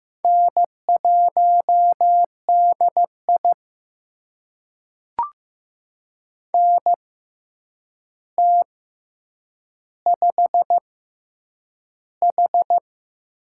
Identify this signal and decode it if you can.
Morse code, heard as a keyed tone, about 15 wpm, spaced as for 8 wpm, 700 Hz. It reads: N1DI NT5H
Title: USA Call Signs - 4 Characters - 6x Character Spacing - 15wpm